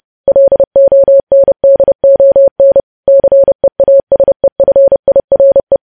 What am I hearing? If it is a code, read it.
Morse code, heard as a keyed tone, about 30 wpm, 550 Hz. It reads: LONDON CEASEFIRE
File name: podcast / News headlines-1x-30x30